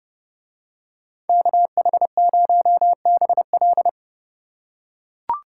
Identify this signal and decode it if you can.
Morse code, heard as a keyed tone, about 31 wpm, 700 Hz. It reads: KH0BL